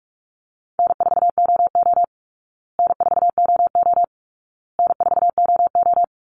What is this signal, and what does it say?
Morse code, heard as a keyed tone, about 44 wpm, 700 Hz. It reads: N4OO N4OO N4OO
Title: Call Signs Repeated 2x - 45wpm